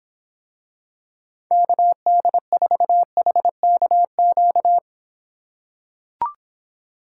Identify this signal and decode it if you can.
Morse code, heard as a keyed tone, about 26 wpm, 700 Hz. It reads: KD4HKQ